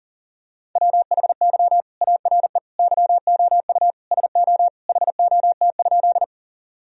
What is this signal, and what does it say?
Morse code, heard as a keyed tone, about 40 wpm, 700 Hz. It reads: WHY ARE YOU SO HOT?
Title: Sentences from Top 1000 Words - Review Part 3 - 40wpm